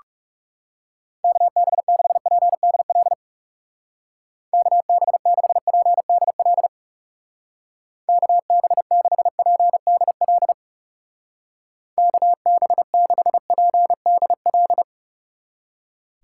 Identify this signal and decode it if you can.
Morse code, heard as a keyed tone, about 36 wpm, 700 Hz. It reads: KB6PDL KB6PDL KB6PDL KB6PDL
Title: Speed Racing - USA Call Signs 30wpm